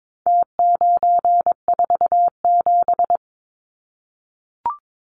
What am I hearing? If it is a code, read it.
Morse code, heard as a keyed tone, about 22 wpm, 700 Hz. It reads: T947
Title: Numbers - Contest Style - 4-Digit 22wpm